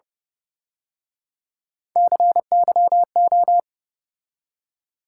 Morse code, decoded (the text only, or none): CYO